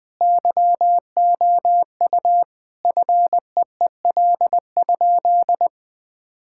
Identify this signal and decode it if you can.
Morse code, heard as a keyed tone, about 21 wpm, 700 Hz. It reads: YOU FEEL?
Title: Sentences with 3-Word Phrases - 20wpm